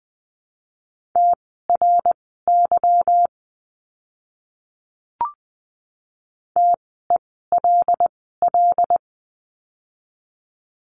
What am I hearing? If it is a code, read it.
Morse code, heard as a keyed tone, about 20 wpm, spaced as for 14 wpm, 700 Hz. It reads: TRY TELL
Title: 160 Lesson 26f - Letter g - Words 20wpm